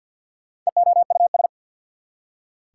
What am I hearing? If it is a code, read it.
Morse code, heard as a keyed tone, about 67 wpm, spaced as for 58 wpm, 690 Hz. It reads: EOUS
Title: Speed-Racing Prefixes and Suffixes 50wpm